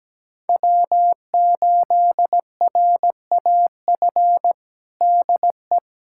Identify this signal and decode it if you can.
Morse code, heard as a keyed tone, about 17 wpm, 700 Hz. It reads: W8RAF DE
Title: USA Callsign DE Callsign - 5 Characters - 17wpm